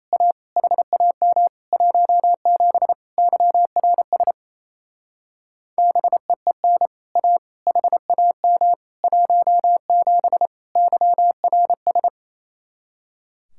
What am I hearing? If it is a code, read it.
Morse code, heard as a keyed tone, about 34 wpm, 700 Hz. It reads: A HAM 17 YRS BEEN A HAM 17 YRS
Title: Speed-Racing Common QSO Phrases - 28wpm